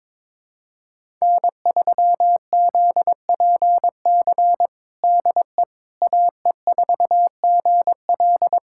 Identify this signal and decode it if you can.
Morse code, heard as a keyed tone, about 22 wpm, 700 Hz. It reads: N3ZPC DE AE4GL
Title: USA Callsign DE Callsign - 5 Characters - 22wpm